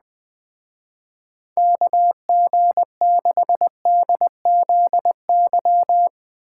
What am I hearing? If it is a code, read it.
Morse code, heard as a keyed tone, about 21 wpm, 700 Hz. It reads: KG6DZY